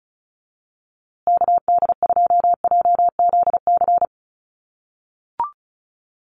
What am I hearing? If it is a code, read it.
Morse code, heard as a keyed tone, about 34 wpm, 700 Hz. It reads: KD2JZC
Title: USA Call Signs Repeated 2x 35wpm